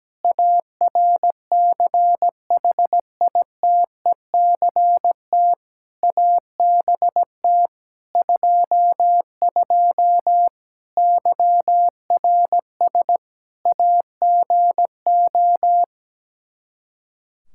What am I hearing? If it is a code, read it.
Morse code, heard as a keyed tone, about 17 wpm, 700 Hz. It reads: ARCHITECT ABT 22 YRS AGO
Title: Practice QSO - Exchange 3 - More Info - 17wpm